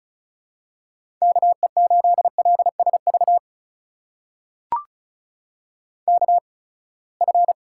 Morse code, heard as a keyed tone, about 36 wpm, spaced as for 24 wpm, 700 Hz. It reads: KE8LSV KF
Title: USA Call Signs - 6 Characters - 8x Character Spacing - 35wpm